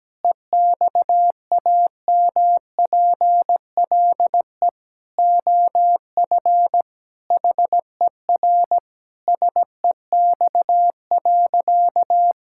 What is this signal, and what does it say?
Morse code, heard as a keyed tone, about 17 wpm, 700 Hz. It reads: EXAMPLE OF HER SEX.
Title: Sentences from Top 1600 Words 17wpm